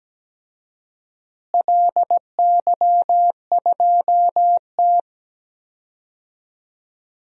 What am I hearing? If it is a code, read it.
Morse code, heard as a keyed tone, about 17 wpm, 700 Hz. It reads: LY2T
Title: DXCC-100 Call Signs - 4 Characters - Repeated 2x - 17wpm